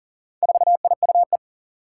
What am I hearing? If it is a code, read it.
Morse code, heard as a keyed tone, about 40 wpm, 700 Hz. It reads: 4IUE
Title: Four Challenging Letter-Number Combinations 40wpm